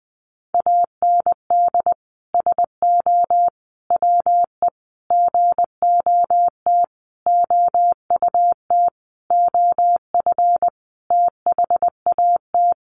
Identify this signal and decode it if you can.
Morse code, heard as a keyed tone, about 20 wpm, 700 Hz. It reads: AND SO WE GOT OUT OF THAT
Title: Sentences from Top 1500 Words 20wpm